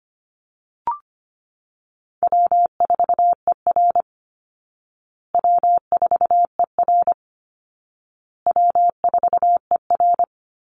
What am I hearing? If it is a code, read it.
Morse code, heard as a keyed tone, about 25 wpm, 700 Hz. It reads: W4ER W4ER W4ER
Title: Easy Call Signs Repeated 3x - 25wpm